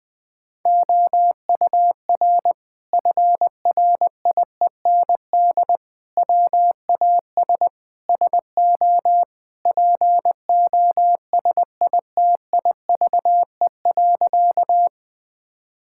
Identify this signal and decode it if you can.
Morse code, heard as a keyed tone, about 20 wpm, 700 Hz. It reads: OUR FRIEND WAS SO POSITIVE.
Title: Sentences from Top 1000 Words - Review Part 5 - 20wpm